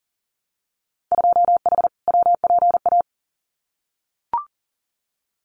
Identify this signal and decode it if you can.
Morse code, heard as a keyed tone, about 40 wpm, 700 Hz. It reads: 2H WPA
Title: Winter Field Day Exchanges 40wpm